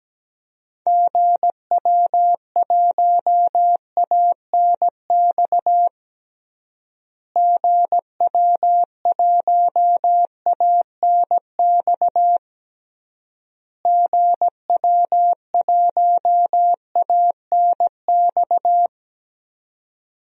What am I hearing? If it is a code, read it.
Morse code, heard as a keyed tone, about 17 wpm, 700 Hz. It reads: GW1ANX GW1ANX GW1ANX
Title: DXCC-100 Call Signs - 6 Characters - Repeated 3x - 17wpm